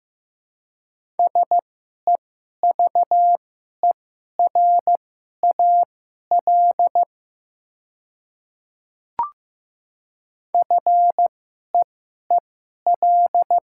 Morse code, heard as a keyed tone, about 15 wpm, spaced as for 11 wpm, 700 Hz. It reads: SEVERAL FEEL
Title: 2x Character Spacing - Top 500 Words - 15wpm